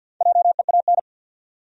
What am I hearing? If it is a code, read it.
Morse code, heard as a keyed tone, about 50 wpm, 700 Hz. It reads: JEAN
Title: USA Common Names - Max 4 Letters - 50wpm